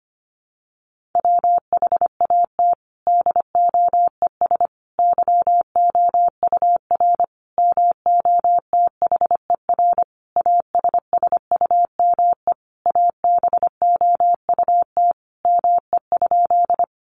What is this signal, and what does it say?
Morse code, heard as a keyed tone, about 25 wpm, 700 Hz. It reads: WHAT DOES YOUR MOTHER ASSUME ABOUT ME?
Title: Sentences from Top 1500 Words 25wpm